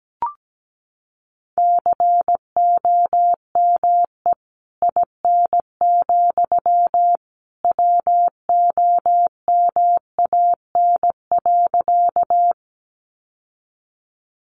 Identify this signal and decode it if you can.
Morse code, heard as a keyed tone, about 17 wpm, 700 Hz. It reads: COME IN, WOMAN.
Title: Sentences from Top 1100 Words 17wpm